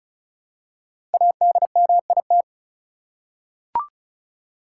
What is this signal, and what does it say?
Morse code, heard as a keyed tone, about 36 wpm, 700 Hz. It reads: ADMIT
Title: Five Letter Words - 35wpm